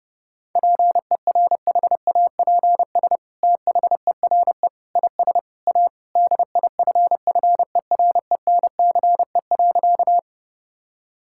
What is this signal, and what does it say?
Morse code, heard as a keyed tone, about 30 wpm, 700 Hz. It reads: PERHAPS THERE IS A DIFFERENCE.
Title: Sentences from Top 1000 Words - Review Part 5 - 30wpm